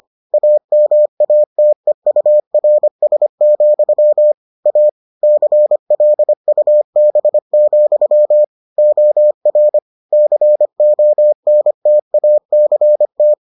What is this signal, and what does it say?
Morse code, heard as a keyed tone, about 25 wpm, 600 Hz. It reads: AMATEURS, A CLUB, OR CONTACT